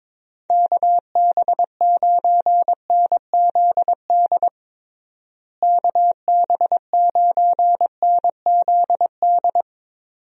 Morse code, heard as a keyed tone, about 22 wpm, 700 Hz. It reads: KB9NZD KB9NZD
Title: USA Call Signs - 6 Characters - Repeated 2x - 22wpm